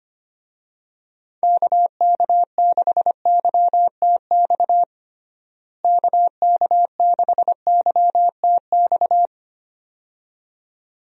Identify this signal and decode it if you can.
Morse code, heard as a keyed tone, about 25 wpm, 700 Hz. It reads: KK6YTX KK6YTX
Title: USA Call Signs - 6 Characters - Repeated 2x - 25wpm